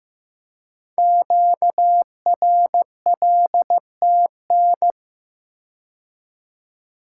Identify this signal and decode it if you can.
Morse code, heard as a keyed tone, about 15 wpm, 700 Hz. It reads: QRLTN
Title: Code Groups - 5 Letters - 15wpm